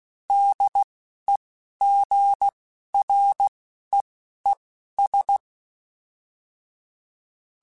Morse code, heard as a keyed tone, about 16 wpm, spaced as for 11 wpm, 800 Hz. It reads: DEGREES
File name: Weather Sets / Weather Sets-mp3s / Weather 4 21 @10